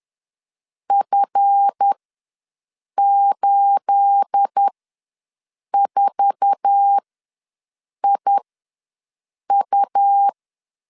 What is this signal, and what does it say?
Morse code, heard as a keyed tone, about 10 wpm, spaced as for 6 wpm, 800 Hz. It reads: F84IU
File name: LEZIONI / 10 SIGNS / 10C DOT ws 10wpm 4213